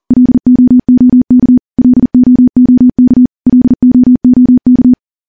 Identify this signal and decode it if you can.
Morse code, heard as a keyed tone, about 43 wpm, 250 Hz. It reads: LOOK LOOK LOOK
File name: morse / morse-training-40WPM